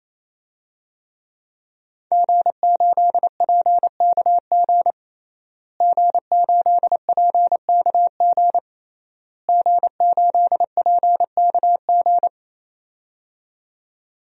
Morse code, heard as a keyed tone, about 28 wpm, 700 Hz. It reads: G8PKG G8PKG G8PKG
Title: DXCC-100 Call Signs - 5 Characters - Repeated 3x - 28wpm